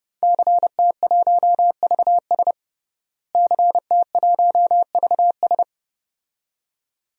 Morse code, heard as a keyed tone, about 32 wpm, 700 Hz. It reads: CT1VS CT1VS
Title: DXCC-100 Call Signs - 5 Characters - Repeated 2x - 30wpm